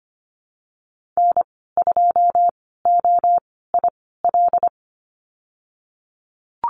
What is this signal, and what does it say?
Morse code, heard as a keyed tone, about 25 wpm, spaced as for 16 wpm, 700 Hz. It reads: N2OIL
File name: Lesson 15g - Number 5 - Call Signs 25wpm